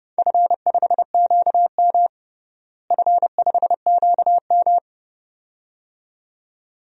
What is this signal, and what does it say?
Morse code, heard as a keyed tone, about 30 wpm, 700 Hz. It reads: F5QM F5QM
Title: DXCC-100 Call Signs - 4 Characters - Repeated 2x - 30wpm